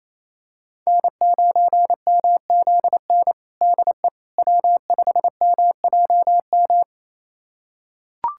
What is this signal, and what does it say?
Morse code, heard as a keyed tone, about 28 wpm, 700 Hz. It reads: N9MZN DE W5MJM
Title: USA Callsign DE Callsign - 5 Characters - 28wpm